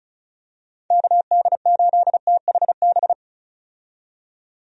Morse code, heard as a keyed tone, about 36 wpm, 700 Hz. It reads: KD8THB
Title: USA Call Signs - 6 Characters - 35wpm